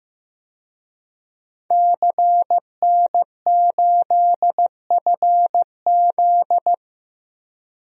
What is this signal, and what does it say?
Morse code, heard as a keyed tone, about 15 wpm, 700 Hz. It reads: CN8FZ
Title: DXCC-100 Call Signs - 5 Characters - 15wpm